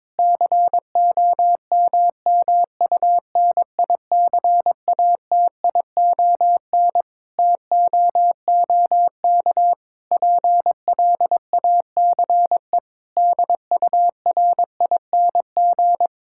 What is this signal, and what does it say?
Morse code, heard as a keyed tone, about 22 wpm, 700 Hz. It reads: COMMUNICATION TOOK PLACE DURING
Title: Sentences from Top 1000 Words - Review Part 6 - 22wpm